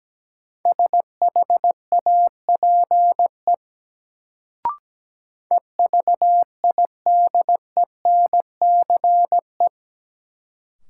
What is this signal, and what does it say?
Morse code, heard as a keyed tone, about 17 wpm, 700 Hz. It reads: SHAPE EVIDENCE
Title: Sets of 5 Words Encore - New Word Drill - 17wpm